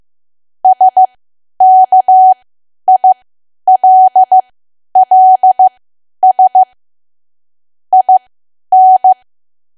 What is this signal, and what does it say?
Morse code, heard as a keyed tone, about 15 wpm, spaced as for 10 wpm, 750 Hz. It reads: SKILLS IN
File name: cwa / W1AW-Level2 / 10WPM